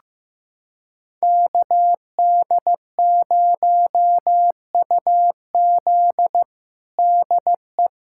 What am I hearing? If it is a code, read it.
Morse code, heard as a keyed tone, about 15 wpm, 700 Hz. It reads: KD0UZ DE
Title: USA Callsign DE Callsign - 5 Characters - 15wpm